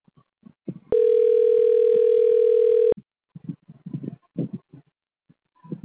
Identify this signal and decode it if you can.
Ringback tone (North America), 440 + 480 Hz.